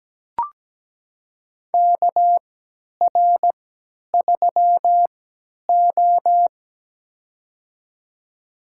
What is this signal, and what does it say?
Morse code, heard as a keyed tone, about 17 wpm, spaced as for 10 wpm, 700 Hz. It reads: KR3O
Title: USA Call Signs - 4 Characters - 3x Character Spacing - 17wpm